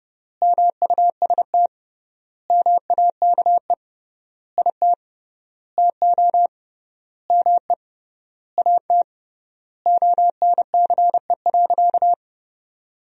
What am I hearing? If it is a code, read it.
Morse code, heard as a keyed tone, about 30 wpm, 700 Hz. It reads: MUST MAKE IT TO ME AT ONCE.
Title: 3x Word-Spacing - Sentences from Top 500 Words - 30wpm